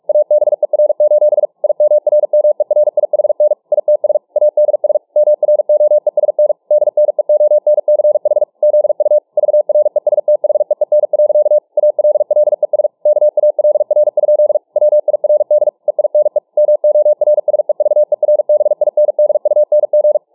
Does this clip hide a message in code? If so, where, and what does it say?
Yes: Morse code from 0.1 to 20.2 s, heard as a keyed tone, about 43 wpm, 600 Hz. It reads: ABER: IMRMERIHN ITS ADS MROESN DNEONCH ZU VRESTHEEN. ALLES KALR? WIRD EINE MORSEVERBINDUNG